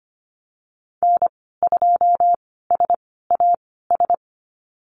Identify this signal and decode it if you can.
Morse code, heard as a keyed tone, about 25 wpm, spaced as for 16 wpm, 700 Hz. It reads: N2SAS
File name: Lesson 15g - Number 5 - Call Signs 25wpm